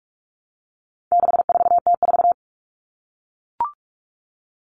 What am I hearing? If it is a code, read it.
Morse code, heard as a keyed tone, about 44 wpm, 700 Hz. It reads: 64T4
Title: All Cut Numbers - 4-Digit 45wpm